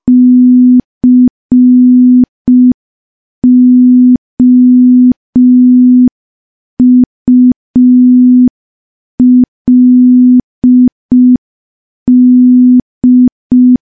Morse code, heard as a keyed tone, about 5 wpm, 250 Hz. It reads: COULD
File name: morse / morse-training-05WPM